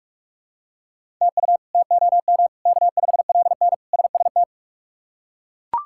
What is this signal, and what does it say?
Morse code, heard as a keyed tone, about 41 wpm, 700 Hz. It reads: TU TOM K5LN SST
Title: K1USN SST QSO Practice - 45wpm